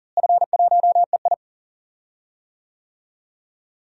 Morse code, heard as a keyed tone, about 40 wpm, 700 Hz. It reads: F1EI
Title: DXCC-100 Call Signs - 4 Characters - Repeated 2x - 40wpm